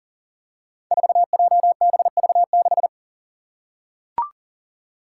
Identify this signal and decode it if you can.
Morse code, heard as a keyed tone, about 40 wpm, 700 Hz. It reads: 4JBV6
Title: Five Challenging Letter-Number Combinations 40wpm